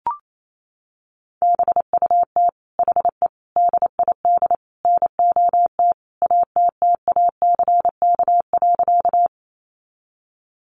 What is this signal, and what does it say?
Morse code, heard as a keyed tone, about 28 wpm, 700 Hz. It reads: BUT HE DID NOT ATTACK.
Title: Sentences from Top 1500 Words 28wpm